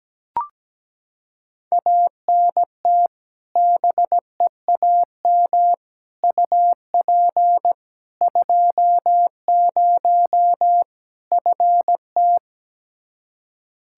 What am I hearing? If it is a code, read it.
Morse code, heard as a keyed tone, about 17 wpm, 700 Hz. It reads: ANT BEAM UP 20 FT
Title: K1USN SST QSO Practice 17wpm